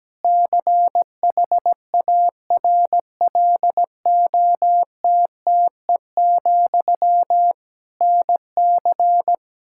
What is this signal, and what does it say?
Morse code, heard as a keyed tone, about 17 wpm, 700 Hz. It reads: CHARLOTTE, NC
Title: Practice QSO - Exchange 1 - Beginning - 17wpm